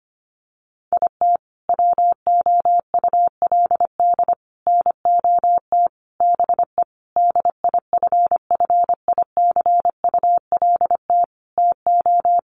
Morse code, heard as a keyed tone, about 25 wpm, 700 Hz. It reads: IT WOULD NOT BE DIFFICULT TO
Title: Sentences from Top 600 Words 25wpm